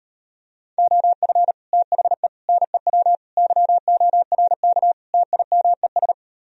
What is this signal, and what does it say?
Morse code, heard as a keyed tone, about 38 wpm, 700 Hz. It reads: OF THE NEW YORK TIMES